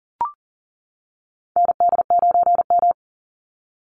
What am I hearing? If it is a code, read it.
Morse code, heard as a keyed tone, about 38 wpm, 700 Hz. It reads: ND9M
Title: Call Signs - 40wpm